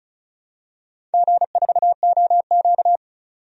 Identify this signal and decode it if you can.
Morse code, heard as a keyed tone, about 36 wpm, 700 Hz. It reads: G4OQ
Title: DXCC-100 Call Signs - 4 Characters - 35wpm